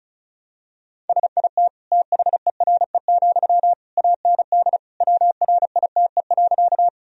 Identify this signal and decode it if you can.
Morse code, heard as a keyed tone, about 34 wpm, 700 Hz. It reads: SIT THERE, AND WRITE.
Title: Sentences from Top 1000 Words - Review Part 1 - 35wpm